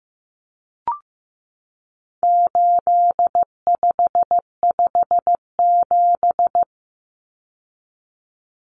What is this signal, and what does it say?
Morse code, heard as a keyed tone, about 15 wpm, 700 Hz. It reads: 8557
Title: Numbers - 4-Digit 15wpm